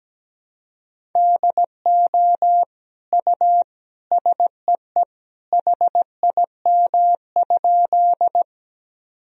Morse code, heard as a keyed tone, about 17 wpm, 700 Hz. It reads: DO U SEE HIM?
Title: Sentences from Top 100 Words and Abbreviations 17wpm